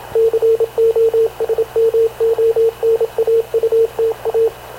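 Morse code, heard as a keyed tone, about 28 wpm, 450 Hz, with radio noise. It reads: COSMONAUTA
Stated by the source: Havana Radio CLA press (PX) broadcast, recorded by Radio Officer Paul Dumesnil aboard mv CNAV QUEST/CZDO.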